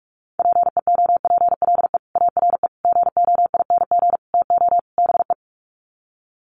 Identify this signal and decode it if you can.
Morse code, heard as a keyed tone, about 44 wpm, 700 Hz. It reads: PEOPLE ARE GOING TO BE
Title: Sets of 5 Words Encore 45wpm